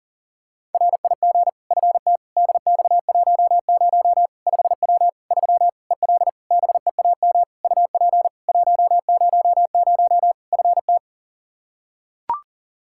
Morse code, heard as a keyed tone, about 37 wpm, 700 Hz. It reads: RIG FT DX10 5W 3 EL BEAM UP 100 FT
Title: Practice QSO - Exchange 2 - Info - 40wpm